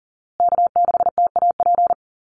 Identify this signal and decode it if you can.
Morse code, heard as a keyed tone, about 40 wpm, 700 Hz. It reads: K6TAP
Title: USA Call Signs Repeated 2x 40wpm